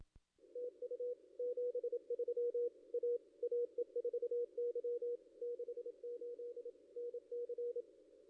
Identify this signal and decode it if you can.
Morse code, heard as a keyed tone, about 26 wpm, 440 Hz. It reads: TU 73 A AE4Y 68 NC
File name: contests / ss-cw-k5tr-op-n5rz -2300